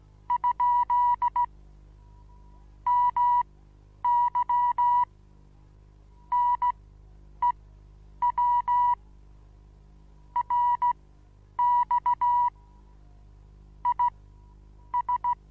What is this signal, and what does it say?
Morse code, heard as a keyed tone, about 16 wpm, spaced as for 9 wpm, 970 Hz. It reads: ? MY NEW RX IS
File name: Onlinelehrgang / morsen / DJ4UF Morsen25